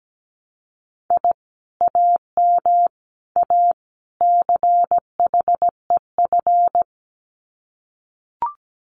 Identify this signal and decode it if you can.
Morse code, heard as a keyed tone, about 17 wpm, 700 Hz. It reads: I AM A CHEF
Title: Common QSO Phrases Encore 17wpm